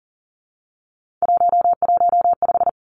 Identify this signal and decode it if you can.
Morse code, heard as a keyed tone, about 42 wpm, 690 Hz. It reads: 115
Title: Three Challenging Letter-Number Combinations 40wpm